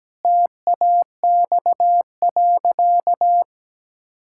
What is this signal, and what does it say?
Morse code, heard as a keyed tone, about 17 wpm, 700 Hz. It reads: TAX.
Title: Sentences from Top 1000 Words - Review Part 6 - 17wpm